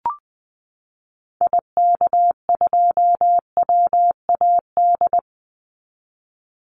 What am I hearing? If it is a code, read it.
Morse code, heard as a keyed tone, about 20 wpm, 700 Hz. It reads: IK2WAD
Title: Call Signs Repeated 2x - 20wpm